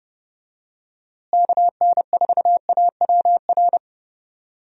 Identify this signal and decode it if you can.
Morse code, heard as a keyed tone, about 30 wpm, 700 Hz. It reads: KN4AWR
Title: USA Call Signs - 6 Characters - Repeated 2x - 30wpm